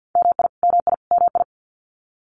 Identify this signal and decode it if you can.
Morse code, heard as a keyed tone, about 50 wpm, 700 Hz. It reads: MI MI MI
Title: US States Repeated 3x 50wpm